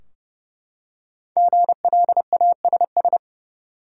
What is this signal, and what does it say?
Morse code, heard as a keyed tone, about 29 wpm, 700 Hz. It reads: GLASS
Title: Five Letter Words - 30wpm